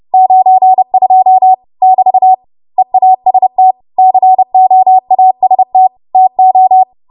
Morse code, heard as a keyed tone, about 30 wpm, 750 Hz. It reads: 92 <BT> EAST COAST TO
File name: Morse / 30WPM